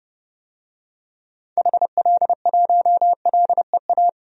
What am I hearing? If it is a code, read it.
Morse code, heard as a keyed tone, about 32 wpm, 700 Hz. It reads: HL1LEA
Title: DXCC-100 Call Signs - 6 Characters - 30wpm